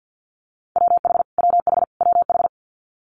Morse code, heard as a keyed tone, about 50 wpm, 690 Hz. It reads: WH WH WH
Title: Multi-letter Phonemes Repeated 3x 50wpm